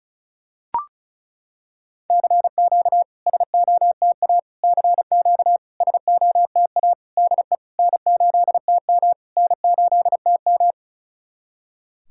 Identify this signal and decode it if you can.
Morse code, heard as a keyed tone, about 34 wpm, 690 Hz. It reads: CQ SOTA CQ SOTA DE N8TM N8TM
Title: SOTA QSO Practice 35wpm